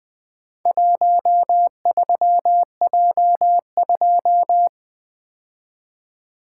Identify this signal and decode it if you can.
Morse code, heard as a keyed tone, about 20 wpm, 700 Hz. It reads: 13J2
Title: Four Letter-Number Combinations - 20wpm